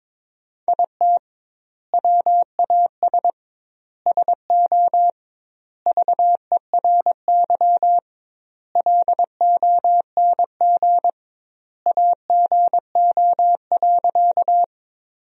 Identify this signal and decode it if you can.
Morse code, heard as a keyed tone, about 21 wpm, 700 Hz. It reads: IT WAS SO VERY LONG AGO.
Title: 2x Word-Spacing - Sentences from Top 500 Words - 22wpm